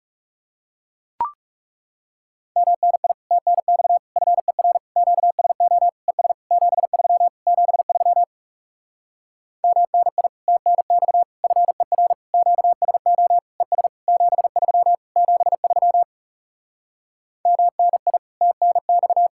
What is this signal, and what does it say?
Morse code, heard as a keyed tone, about 40 wpm, 700 Hz. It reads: MNI TNX FER QSO ES 73 73 MNI TNX FER QSO ES 73 73 MNI TNX